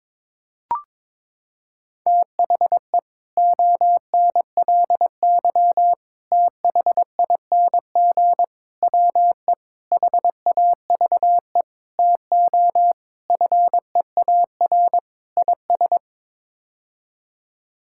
Morse code, heard as a keyed tone, about 22 wpm, 700 Hz. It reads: THE ONLY THING WE HAVE TO FEAR IS
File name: practice / 22wpm / Sets of 8 Words - 22wpm